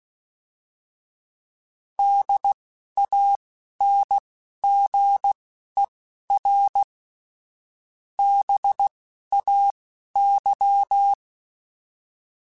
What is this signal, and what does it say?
Morse code, heard as a keyed tone, about 16 wpm, spaced as for 11 wpm, 800 Hz. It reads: DANGER BAY
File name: fromSMs / FromSMs mp3s / Cities@10